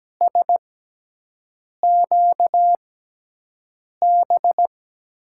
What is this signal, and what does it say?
Morse code, heard as a keyed tone, about 17 wpm, 700 Hz. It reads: S Q B
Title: USA Call Signs - 6 Characters - 6x Character Spacing - 17wpm